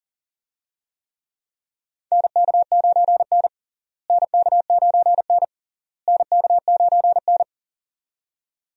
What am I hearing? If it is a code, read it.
Morse code, heard as a keyed tone, about 40 wpm, 700 Hz. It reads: NK9N NK9N NK9N
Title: USA Call Signs - 4 Characters - Repeated 3x - 40wpm